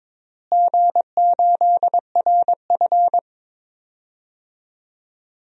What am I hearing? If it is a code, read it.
Morse code, heard as a keyed tone, about 22 wpm, 700 Hz. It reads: G8RF
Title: DXCC-100 Call Signs - 4 Characters - Repeated 3x - 22wpm